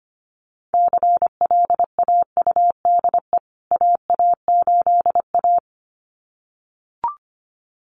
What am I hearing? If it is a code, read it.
Morse code, heard as a keyed tone, about 25 wpm, 700 Hz. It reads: CLAUDE AA8A
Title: CWT Exchange 25wpm